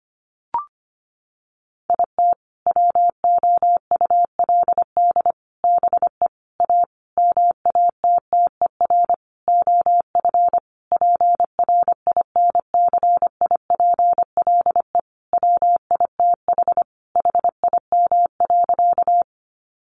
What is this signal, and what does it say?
Morse code, heard as a keyed tone, about 25 wpm, 700 Hz. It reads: IT WOULD BE A MATTER OF PRINCIPLE WITH HIM.
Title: Sentences from Top 1400 Words 25wpm